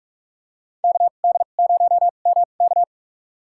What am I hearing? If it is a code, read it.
Morse code, heard as a keyed tone, about 44 wpm, spaced as for 32 wpm, 700 Hz. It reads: KD0MK
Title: USA Call Signs - 5 Characters - 2x Character Spacing - 45wpm